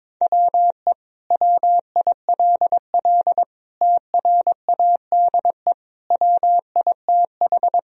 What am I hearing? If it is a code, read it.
Morse code, heard as a keyed tone, about 21 wpm, 700 Hz. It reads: WE WILL TRADE WITH
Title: Sentences from Top 1000 Words - Review Part 4 - 22wpm